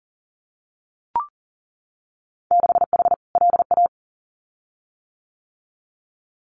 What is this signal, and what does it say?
Morse code, heard as a keyed tone, about 39 wpm, 700 Hz. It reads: 6H LA
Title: Winter Field Day Exchanges Encore 40wpm